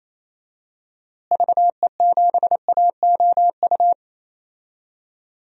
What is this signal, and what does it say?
Morse code, heard as a keyed tone, about 28 wpm, 700 Hz. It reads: VE7AOU
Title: DXCC-100 Call Signs - 6 Characters - Repeated 2x - 28wpm